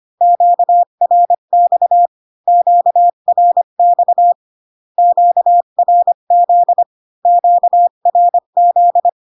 Morse code, heard as a keyed tone, about 25 wpm, 700 Hz. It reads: QRX QRX QRZ QRZ